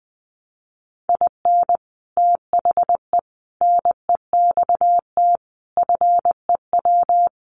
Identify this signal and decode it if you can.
Morse code, heard as a keyed tone, about 19 wpm, 700 Hz. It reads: IN THE NEXT FEW
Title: Sets of 5 Words Encore2 20wpm